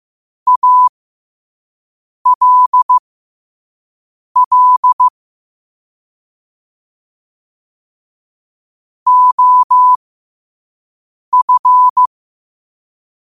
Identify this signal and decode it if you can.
Morse code, heard as a keyed tone, about 15 wpm, spaced as for 5 wpm, 1000 Hz. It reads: ALL OF
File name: qotdfiles / 5wpm